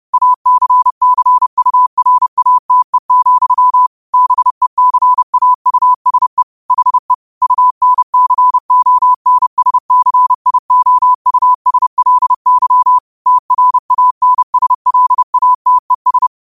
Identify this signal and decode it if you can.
Morse code, heard as a keyed tone, about 30 wpm, 1000 Hz. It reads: ACCURATE, BECAUSE HE UNCONSCIOUSLY TRANSLATES
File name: qotdfiles / 30wpm